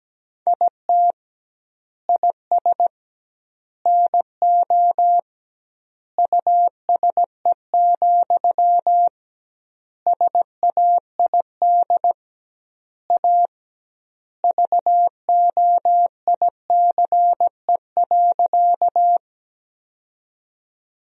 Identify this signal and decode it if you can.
Morse code, heard as a keyed tone, about 17 wpm, 700 Hz. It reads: IT IS NO USE, SAID A VOICE.
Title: 2x Word-Spacing - Sentences from Top 500 Words - 17wpm